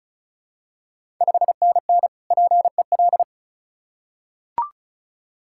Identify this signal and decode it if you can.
Morse code, heard as a keyed tone, about 36 wpm, 700 Hz. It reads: 5NN PEL
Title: Ontario QSO Party Exchange 35wpm